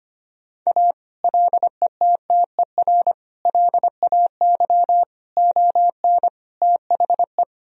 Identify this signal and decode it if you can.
Morse code, heard as a keyed tone, about 25 wpm, 700 Hz. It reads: A LETTER LAY ON THE
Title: Sentences from Top 1000 Words - Review Part 2 - 25wpm